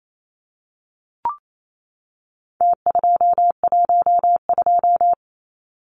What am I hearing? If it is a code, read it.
Morse code, heard as a keyed tone, about 28 wpm, 700 Hz. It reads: T212
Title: Numbers - Contest Style - 4-Digit 28wpm